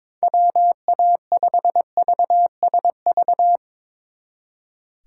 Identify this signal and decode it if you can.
Morse code, heard as a keyed tone, about 24 wpm, 700 Hz. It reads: WA5VSV
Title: Speed Racing - USA Call Signs 22wpm